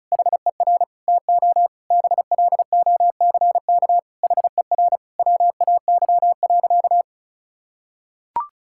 Morse code, heard as a keyed tone, about 36 wpm, 700 Hz. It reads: HER TO BLOCK HER WAY.
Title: Sentences from Top 1600 Words 35wpm